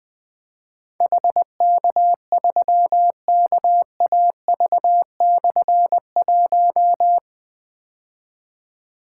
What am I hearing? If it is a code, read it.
Morse code, heard as a keyed tone, about 20 wpm, 700 Hz. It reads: HK3KAV/1
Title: Difficult Call Signs 20wpm